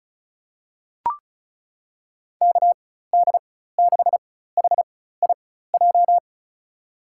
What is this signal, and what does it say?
Morse code, heard as a keyed tone, about 34 wpm, spaced as for 16 wpm, 700 Hz. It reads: KD6HIJ
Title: USA Call Signs - 6 Characters - 4x Character Spacing - 35wpm